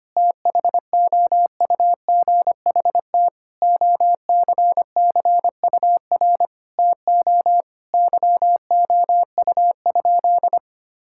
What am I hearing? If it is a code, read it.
Morse code, heard as a keyed tone, about 25 wpm, 700 Hz. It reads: THOUGHT OCCUR TO YOU?
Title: Sentences from Top 1000 Words - Review Part 6 - 25wpm